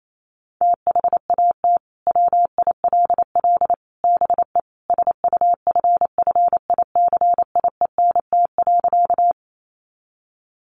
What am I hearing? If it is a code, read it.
Morse code, heard as a keyed tone, about 28 wpm, 700 Hz. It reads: THAT WILL BE SUFFICIENT.
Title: Sentences from Top 1400 Words 28wpm